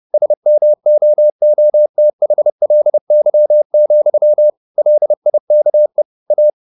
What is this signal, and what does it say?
Morse code, heard as a keyed tone, about 30 wpm, 600 Hz. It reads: SMOOTHLY, LIKE A
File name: the old man and the sea / the old man and the sea 1 30wpm